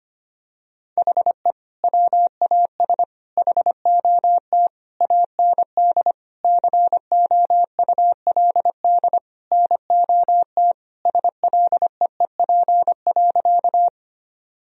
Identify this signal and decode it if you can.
Morse code, heard as a keyed tone, about 25 wpm, 700 Hz. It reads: HE WAS HOT AND COULD NOT SLEEP.
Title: Sentences from Top 1000 Words - Review Part 3 - 25wpm